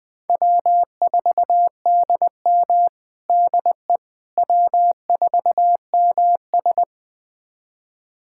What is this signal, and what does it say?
Morse code, heard as a keyed tone, about 20 wpm, 700 Hz. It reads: W4DM DE W4MS
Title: USA Callsign DE Callsign - 4 Characters - 20wpm